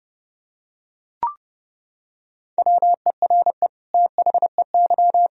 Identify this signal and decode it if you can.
Morse code, heard as a keyed tone, about 30 wpm, 700 Hz. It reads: WERE THEY
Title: Sentences from Top 1000 Words - Review Part 3 - 30wpm